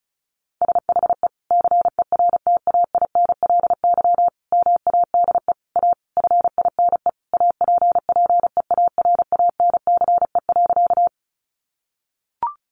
Morse code, heard as a keyed tone, about 34 wpm, 700 Hz. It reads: SHE CERTAINLY MADE A FINE APPEARANCE.
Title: Sentences from Top 1100 Words 35wpm